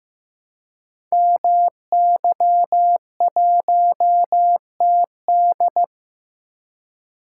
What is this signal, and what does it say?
Morse code, heard as a keyed tone, about 15 wpm, 700 Hz. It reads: MY1TD
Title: Code Groups - 5 Letter:Numbers - 15wpm